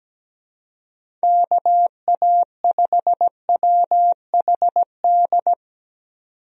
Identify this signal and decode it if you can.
Morse code, heard as a keyed tone, about 17 wpm, 700 Hz. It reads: KA5WHD
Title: USA Call Signs - 6 Characters - 17wpm